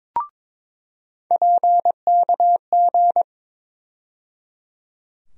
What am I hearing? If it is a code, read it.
Morse code, heard as a keyed tone, about 22 wpm, 700 Hz. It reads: PKG
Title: USA Call Sign Postfixes - 3 Letters 22wpm